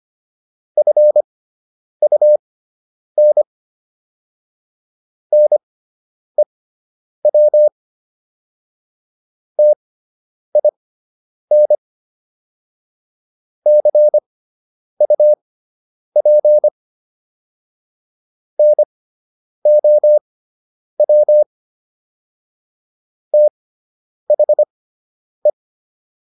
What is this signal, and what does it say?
Morse code, heard as a keyed tone, about 25 wpm, spaced as for 9 wpm, 600 Hz. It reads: FUN NEW TIN CUP NOW THE